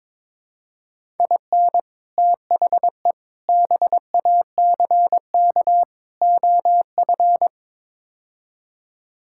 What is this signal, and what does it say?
Morse code, heard as a keyed tone, about 22 wpm, 700 Hz. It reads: IN THE BACK OF
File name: practice / 22wpm / Build A Phrase - 5-Word Phrases - 22wpm